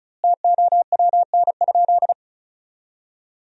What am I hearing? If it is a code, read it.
Morse code, heard as a keyed tone, about 36 wpm, 700 Hz. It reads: TOWN?
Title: 2x Word-Spacing - Sentences from Top 500 Words - 35wpm